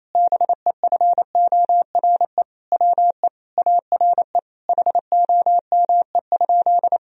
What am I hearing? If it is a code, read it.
Morse code, heard as a keyed tone, about 28 wpm, 700 Hz. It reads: BEFORE WE ARE HOME?
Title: Sentences from Top 1000 Words - Review Part 1 - 28wpm